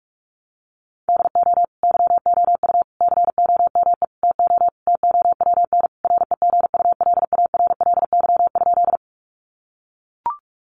Morse code, heard as a keyed tone, about 44 wpm, 700 Hz. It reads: DO YOU COME TO TOWN REGULARLY?
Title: Sentences from Top 1400 Words 45wpm